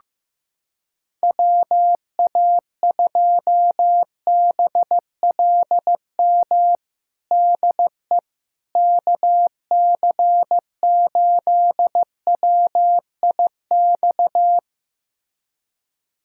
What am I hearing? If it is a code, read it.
Morse code, heard as a keyed tone, about 15 wpm, 700 Hz. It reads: WA2BLM DE KC8WIX
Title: USA Callsign DE Callsign 15wpm